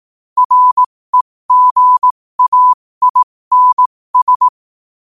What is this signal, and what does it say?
Morse code, heard as a keyed tone, about 18 wpm, spaced as for 15 wpm, 1000 Hz. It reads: REGAINS
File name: qotdfiles / 15wpm